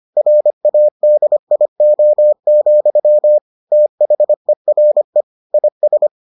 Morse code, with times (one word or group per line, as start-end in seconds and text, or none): 0.2-3.4 RADIO,
3.7-5.2 THERE
5.5-6.1 IS